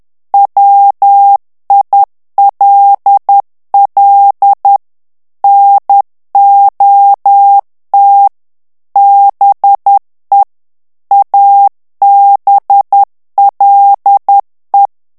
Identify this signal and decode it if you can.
Morse code, heard as a keyed tone, about 10 wpm, 800 Hz. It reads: WILL NOT BE ABLE
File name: ANGLAIS texte clair 3 vitesse 10 mots minute